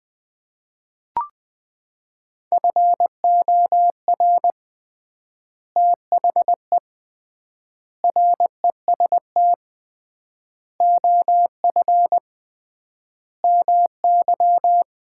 Morse code, heard as a keyed tone, about 21 wpm, 700 Hz. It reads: FOR THE REST OF MY
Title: 3x Word-Spacing - Sets of 5 Words - 20wpm